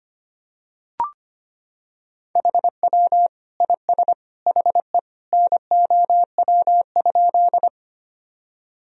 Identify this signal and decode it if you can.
Morse code, heard as a keyed tone, about 25 wpm, 700 Hz. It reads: HW IS HE NOW?
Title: Sentences from Top 100 Words and Abbreviations 25wpm